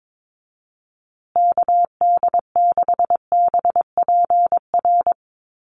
Morse code, heard as a keyed tone, about 22 wpm, 700 Hz. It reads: KD6BPR
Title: USA Call Signs Repeated 2x 22wpm